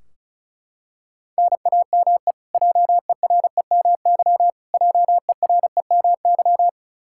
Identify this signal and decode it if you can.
Morse code, heard as a keyed tone, about 34 wpm, 700 Hz. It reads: NAME JEREMY JEREMY
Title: Practice QSO - Exchange 1 - Beginning - 35wpm